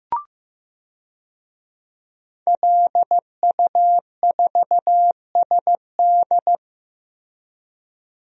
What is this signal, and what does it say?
Morse code, heard as a keyed tone, about 15 wpm, 700 Hz. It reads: LU4SD
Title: DXCC-100 Call Signs - 5 Characters - Repeated 2x - 15wpm